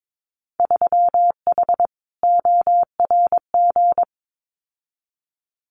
Morse code, heard as a keyed tone, about 22 wpm, 700 Hz. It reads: 3H ORG
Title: Winter Field Day Exchanges Encore 22wpm